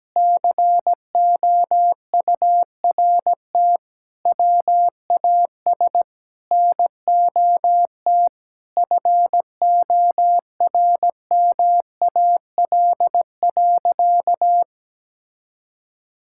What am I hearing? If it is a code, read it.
Morse code, heard as a keyed tone, about 17 wpm, 700 Hz. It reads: COURT WAS NOT FORMAL.
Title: Sentences from Top 1600 Words 17wpm